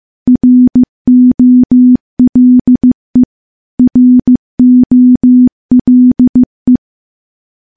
Morse code, heard as a keyed tone, about 15 wpm, 250 Hz. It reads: ROLE ROLE
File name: morse / morse-training-15WPM